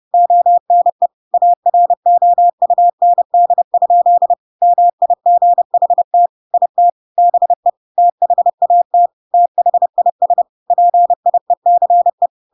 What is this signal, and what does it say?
Morse code, heard as a keyed tone, about 30 wpm, 700 Hz. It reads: ONE AROUND? MIGHT IT BE THAT THIS PIECE
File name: Pinocchio / Pinocchio ch1 30 30 ik1hge